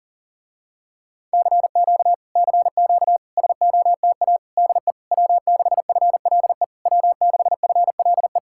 Morse code, heard as a keyed tone, about 34 wpm, 700 Hz. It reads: CQ CQ SOTA DE W6FLE W6FLE
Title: Speed-Racing Common QSO Phrases Encore - 30wpm